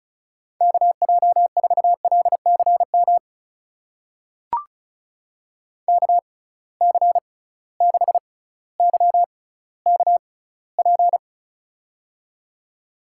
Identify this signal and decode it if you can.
Morse code, heard as a keyed tone, about 36 wpm, spaced as for 18 wpm, 700 Hz. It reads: KJ4LCM KC6YKP
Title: USA Call Signs - 6x Character Spacing - 35wpm